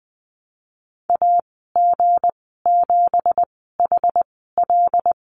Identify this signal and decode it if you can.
Morse code, heard as a keyed tone, about 19 wpm, spaced as for 14 wpm, 700 Hz. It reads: AG7HL
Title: 167 Lesson 27g - Number 7 - Call Signs 20wpm